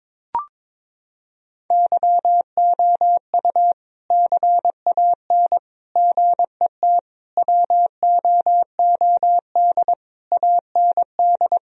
Morse code, heard as a keyed tone, about 22 wpm, 700 Hz. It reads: YOU CAN GET WOOD AND
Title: Sentences from Top 1000 Words - Review Part 2 - 22wpm